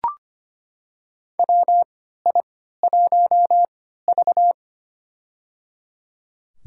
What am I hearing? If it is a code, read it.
Morse code, heard as a keyed tone, about 25 wpm, spaced as for 14 wpm, 700 Hz. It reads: WI1V
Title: USA Call Signs - 4 Characters - 3x Character Spacing - 25wpm